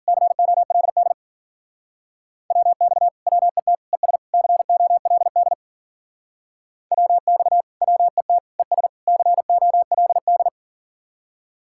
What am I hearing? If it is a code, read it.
Morse code, heard as a keyed tone, about 50 wpm, 700 Hz. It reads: COLD WX WET ES COLD WX WET ES COLD
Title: Speed-Racing Common QSO Phrases - 40wpm